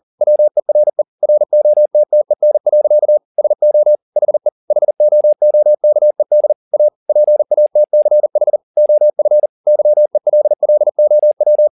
Morse code, heard as a keyed tone, about 40 wpm, 600 Hz. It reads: WERE ROTTEN. SO HE HOOKED A PATCH OF YELLOW
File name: the old man and the sea 2 40wpm